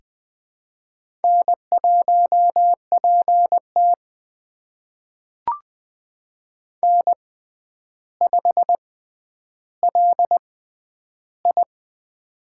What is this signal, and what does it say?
Morse code, heard as a keyed tone, about 20 wpm, spaced as for 10 wpm, 700 Hz. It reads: N1PT N5LI
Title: USA Call Signs - 6x Character Spacing - 20wpm